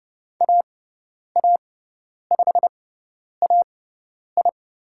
Morse code, heard as a keyed tone, about 30 wpm, spaced as for 10 wpm, 700 Hz. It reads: AA5AI
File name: Lesson 16g - Letter c - Call Signs 30wpm